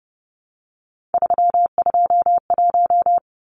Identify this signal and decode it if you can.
Morse code, heard as a keyed tone, about 30 wpm, 700 Hz. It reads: 321
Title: Numbers - 3-Digit 30wpm